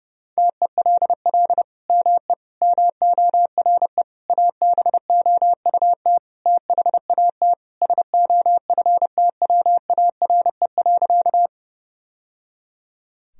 Morse code, heard as a keyed tone, about 30 wpm, 700 Hz. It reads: TELL ME MORE ABOUT THAT SOFTWARE.
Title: Sentences from Top 1000 Words - Review Part 3 - 30wpm